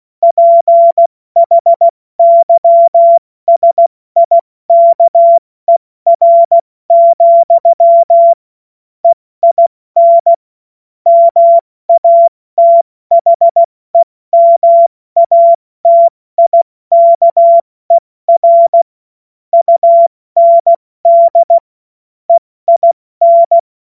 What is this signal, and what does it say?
Morse code, heard as a keyed tone, about 16 wpm, spaced as for 14 wpm, 680 Hz. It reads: PHYSIKER, EIN MATHEMATIKER UND EIN